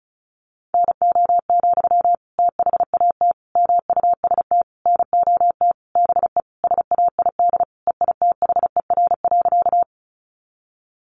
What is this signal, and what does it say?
Morse code, heard as a keyed tone, about 34 wpm, 700 Hz. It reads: NO, THAT MUST NOT BE SAID EITHER.
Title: Sentences from Top 600 Words 35wpm